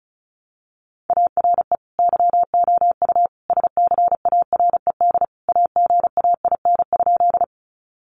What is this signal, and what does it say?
Morse code, heard as a keyed tone, about 34 wpm, 700 Hz. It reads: ARE YOU SCARED AGAIN?
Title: Sentences from Top 700 Words 35wpm